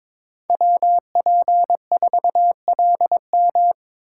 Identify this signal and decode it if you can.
Morse code, heard as a keyed tone, about 22 wpm, 700 Hz. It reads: WP4LM
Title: USA Call Signs - 5 Characters - 4x Character Spacing - 22wpm